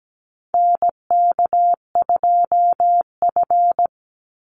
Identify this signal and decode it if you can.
Morse code, heard as a keyed tone, about 17 wpm, 700 Hz. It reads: NK2F
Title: Call Signs Repeated 3x - 17wpm